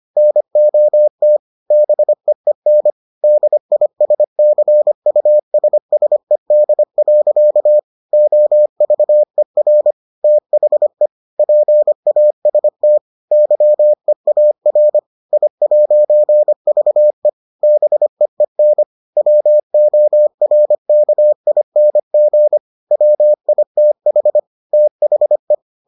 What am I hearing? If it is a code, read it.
Morse code, heard as a keyed tone, about 25 wpm, 600 Hz. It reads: NOT BEEN DISCUSSED. OVER THE PAST YEAR I'VE BEEN WORKING WITH THE